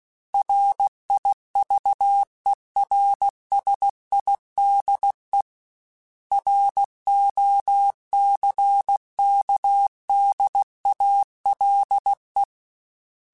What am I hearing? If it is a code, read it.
Morse code, heard as a keyed tone, about 16 wpm, 800 Hz. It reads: RIVERSIDE ROCKDALE
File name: fromSMs / FromSMs mp3s / Cities@15